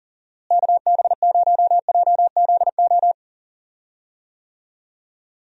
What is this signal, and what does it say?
Morse code, heard as a keyed tone, about 38 wpm, 700 Hz. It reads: KB0JZO
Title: USA Call Signs - 6 Characters - Repeated 2x - 40wpm